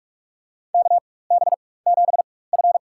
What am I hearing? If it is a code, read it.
Morse code, heard as a keyed tone, about 44 wpm, spaced as for 21 wpm, 700 Hz. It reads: KB7F
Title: USA Call Signs - 4 Characters - 4x Character Spacing - 45wpm